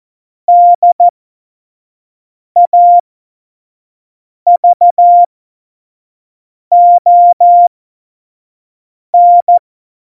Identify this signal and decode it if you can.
Morse code, heard as a keyed tone, about 14 wpm, spaced as for 5 wpm, 700 Hz. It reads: DAVON
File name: Morsekurs / Witze / Witz 22. Januar 2019, 25 F70